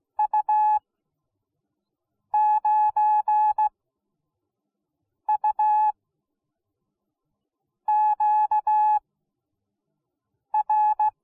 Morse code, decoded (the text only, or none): U9UQR